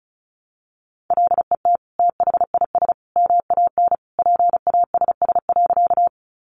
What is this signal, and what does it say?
Morse code, heard as a keyed tone, about 34 wpm, 700 Hz. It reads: LET THIS MAN PASS.
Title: Sentences from Top 400 Words 35wpm